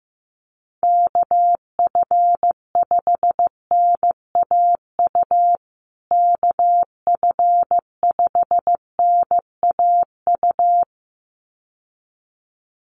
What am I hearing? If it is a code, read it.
Morse code, heard as a keyed tone, about 15 wpm, 700 Hz. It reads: KF5NAU KF5NAU
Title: USA Call Signs Repeated 2x 15wpm